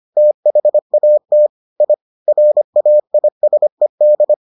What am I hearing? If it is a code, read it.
Morse code, heard as a keyed tone, about 25 wpm, 600 Hz. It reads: THAT I RAISED